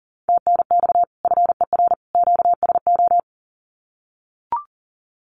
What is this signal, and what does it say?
Morse code, heard as a keyed tone, about 38 wpm, 700 Hz. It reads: TNX FER QSO
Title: Common QSO Phrases 40wpm